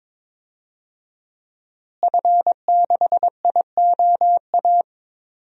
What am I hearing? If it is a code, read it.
Morse code, heard as a keyed tone, about 22 wpm, 700 Hz. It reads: F6IOA
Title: DXCC-100 Call Signs - 5 Characters - Repeated 3x - 22wpm